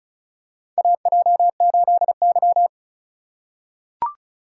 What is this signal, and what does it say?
Morse code, heard as a keyed tone, about 34 wpm, 700 Hz. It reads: AJ8Y
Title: USA Call Signs - 4 Characters - 3x Character Spacing - 35wpm